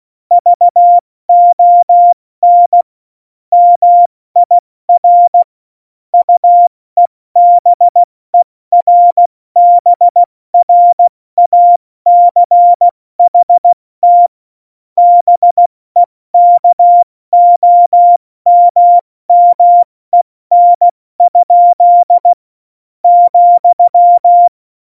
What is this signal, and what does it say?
Morse code, heard as a keyed tone, about 16 wpm, 700 Hz. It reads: VON MIR UEBERBRACHT BEKOMMEN? ,
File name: Morsekurs / Witze / Witz 22. September 2020, 70 F80